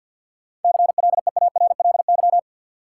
Morse code, heard as a keyed tone, about 50 wpm, 700 Hz. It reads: CLEARLY
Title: Seven Letter Words - 50wpm